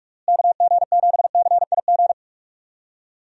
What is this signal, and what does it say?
Morse code, heard as a keyed tone, about 46 wpm, 700 Hz. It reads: KG7CIG
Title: USA Call Signs - 6 Characters - Repeated 2x - 45wpm